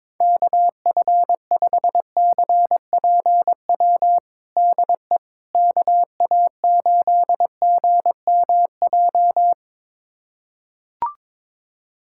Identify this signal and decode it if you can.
Morse code, heard as a keyed tone, about 22 wpm, 700 Hz. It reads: KF5CPW DE KA8GMJ
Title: USA Callsign DE Callsign - 6 Characters - 22wpm